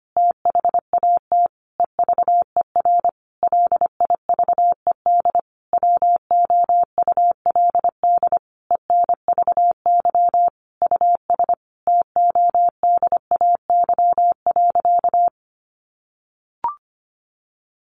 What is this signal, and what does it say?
Morse code, heard as a keyed tone, about 25 wpm, 700 Hz. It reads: THAT EVER LIVED WOULD ENVY US TODAY.
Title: A Sound Of Thunder 25wpm